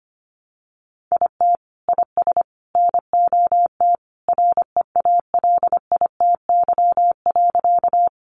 Morse code, heard as a keyed tone, about 25 wpm, 700 Hz. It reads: IT IS NOT REALITY.
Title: Sentences from Top 1200 Words 25wpm